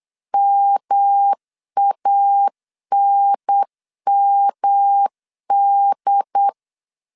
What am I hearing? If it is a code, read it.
Morse code, heard as a keyed tone, about 8 wpm, 800 Hz. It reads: MANMD
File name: CW / DET / 1 3 1500C S B 8wpm 4141